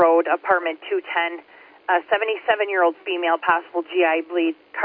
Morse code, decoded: none

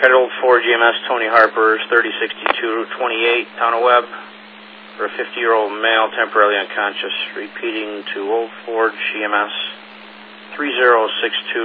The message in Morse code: none

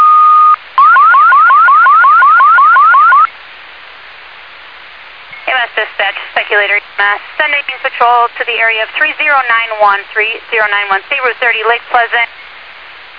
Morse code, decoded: none

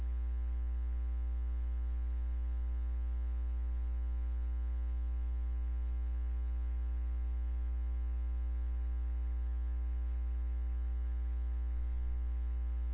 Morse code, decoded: none

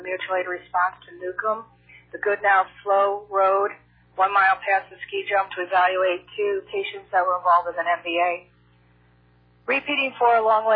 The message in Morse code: none

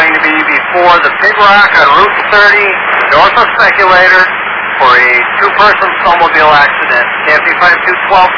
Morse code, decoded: none

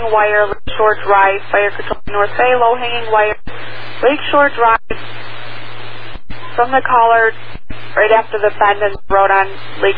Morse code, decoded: none

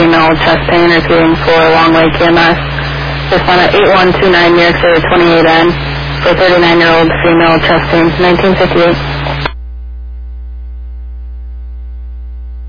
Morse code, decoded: none

